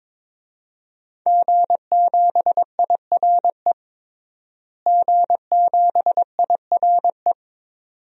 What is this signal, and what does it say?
Morse code, heard as a keyed tone, about 22 wpm, 700 Hz. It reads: G7IRE G7IRE
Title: DXCC-100 Call Signs - 5 Characters - Repeated 2x - 22wpm